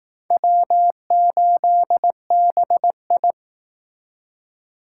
Morse code, heard as a keyed tone, about 19 wpm, 700 Hz. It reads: W8BI
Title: Speed Racing - USA 1x2 Call Signs 15wpm